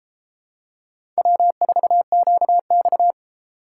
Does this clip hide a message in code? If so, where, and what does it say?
Yes: Morse code from 1.2 to 3.1 s, heard as a keyed tone, about 32 wpm, 700 Hz. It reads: W4QX